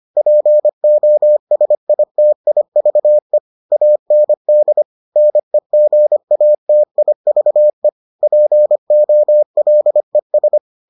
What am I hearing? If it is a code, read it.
Morse code, heard as a keyed tone, about 25 wpm, 600 Hz. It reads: POSITIVE AND NEGATIVE POLES